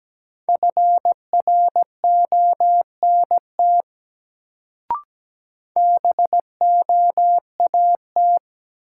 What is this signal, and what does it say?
Morse code, heard as a keyed tone, about 17 wpm, 700 Hz. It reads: FRONT BOAT
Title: Sentences from Top 500 Words - New Word Drill - 17wpm